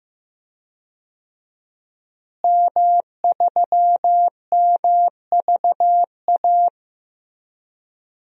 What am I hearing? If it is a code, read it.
Morse code, heard as a keyed tone, about 15 wpm, 700 Hz. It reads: M3MVA
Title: DXCC-100 Call Signs - 5 Characters - Repeated 2x - 15wpm